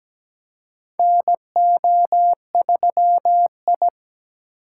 Morse code, heard as a keyed tone, about 17 wpm, 700 Hz. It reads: NO3I
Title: USA Call Signs - 4 Characters - 3x Character Spacing - 17wpm